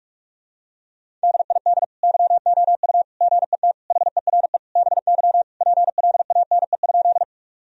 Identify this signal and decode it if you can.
Morse code, heard as a keyed tone, about 46 wpm, 700 Hz. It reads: DID YOU GET HERE BY PLANE?
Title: Sentences from Top 1000 Words - Review Part 3 - 45wpm